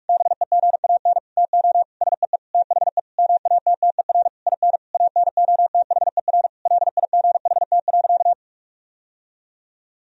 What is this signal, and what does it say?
Morse code, heard as a keyed tone, about 41 wpm, 700 Hz. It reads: BEGAN TO SEE THE MATTER IN ANOTHER LIGHT.
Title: Sentences from Top 1000 Words - Review Part 1 - 45wpm